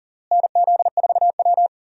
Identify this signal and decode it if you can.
Morse code, heard as a keyed tone, about 40 wpm, 700 Hz. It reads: NZ4W
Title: USA Call Signs - 4 Characters - Repeated 3x - 40wpm